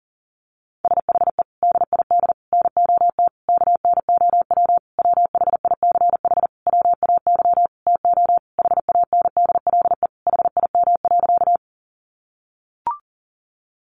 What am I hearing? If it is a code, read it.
Morse code, heard as a keyed tone, about 40 wpm, 700 Hz. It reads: SHE DID NOT KNOW WHICH WAY TO HANDLE HIM.
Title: Sentences from Top 1100 Words 40wpm